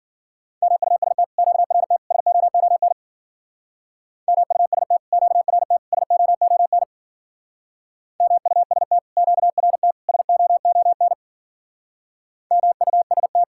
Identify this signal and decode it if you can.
Morse code, heard as a keyed tone, about 51 wpm, 700 Hz. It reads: MUST QRT SOON MUST QRT SOON MUST QRT SOON MUST